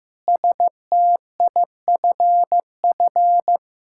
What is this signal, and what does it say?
Morse code, heard as a keyed tone, about 15 wpm, 700 Hz. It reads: STIFF
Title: Pangram Challenge v2 - 15wpm